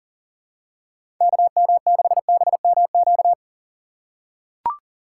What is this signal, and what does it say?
Morse code, heard as a keyed tone, about 40 wpm, 700 Hz. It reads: KM6BMQ
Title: USA Call Signs - 6 Characters - 4x Character Spacing - 40wpm